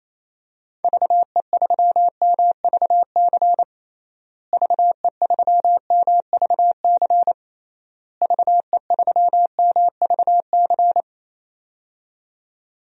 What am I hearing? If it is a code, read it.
Morse code, heard as a keyed tone, about 28 wpm, 700 Hz. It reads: VE3MVC VE3MVC VE3MVC
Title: DXCC-100 Call Signs - 6 Characters - Repeated 3x - 28wpm